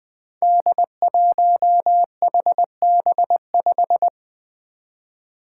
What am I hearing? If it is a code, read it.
Morse code, heard as a keyed tone, about 20 wpm, 700 Hz. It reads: D1HB5
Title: Five Challenging Letter-Number Combinations 20wpm